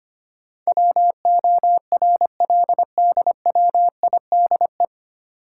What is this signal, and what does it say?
Morse code, heard as a keyed tone, about 25 wpm, 700 Hz. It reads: WORLDWIDE
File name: practice / 25wpm / 9 Letter Words - First 3 Letters Given - 25wpm